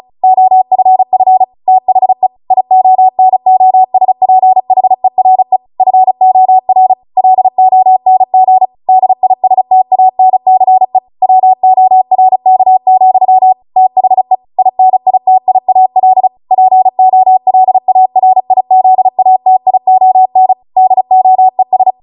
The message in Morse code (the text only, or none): OFF THE IONOSPHERE FOR LONG DISTANCE WORK, THE INITIAL POLARIZATION DOES